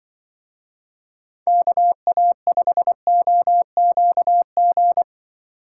Morse code, heard as a keyed tone, about 24 wpm, 700 Hz. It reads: KA5OQG
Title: Speed Racing - USA Call Signs 20wpm